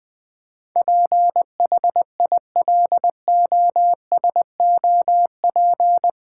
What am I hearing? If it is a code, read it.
Morse code, heard as a keyed tone, about 20 wpm, 700 Hz. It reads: PHILOSOP
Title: Build A Word - 4 Letters At A Time - 11-Letter Words - 20wpm